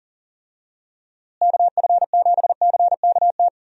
Morse code, heard as a keyed tone, about 40 wpm, 700 Hz. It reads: KF7CKT
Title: USA Call Signs - 6 Characters - Repeated 2x - 40wpm